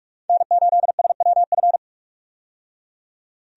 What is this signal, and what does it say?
Morse code, heard as a keyed tone, about 48 wpm, 700 Hz. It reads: N8SWF
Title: USA Call Signs - 5 Characters - 45wpm